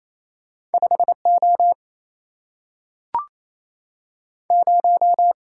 Morse code, heard as a keyed tone, about 28 wpm, 700 Hz. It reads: 5O 0
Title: Single-Letter Single-Number 28wpm